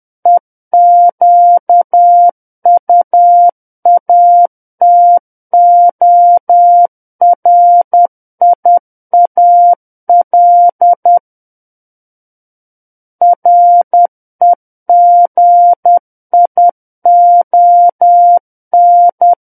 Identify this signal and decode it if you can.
Morse code, heard as a keyed tone, about 10 wpm, 700 Hz. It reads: EQUATORIAL REGION
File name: Book One / 01-Chapter 1 - 10 WPM